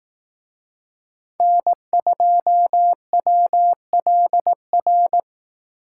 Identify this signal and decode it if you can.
Morse code, heard as a keyed tone, about 18 wpm, 700 Hz. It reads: N2WLR
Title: Speed Racing - USA 1x3 Call Signs 15wpm